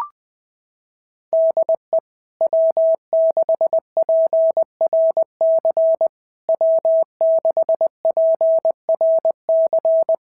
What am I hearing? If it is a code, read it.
Morse code, heard as a keyed tone, about 19 wpm, 650 Hz. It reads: DE W6PRC W6PRC
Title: SOTA QSO Practice 20wpm